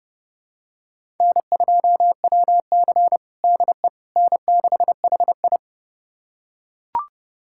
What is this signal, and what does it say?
Morse code, heard as a keyed tone, about 32 wpm, 700 Hz. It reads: N2WC DE N6HI
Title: USA Callsign DE Callsign - 4 Characters - 30wpm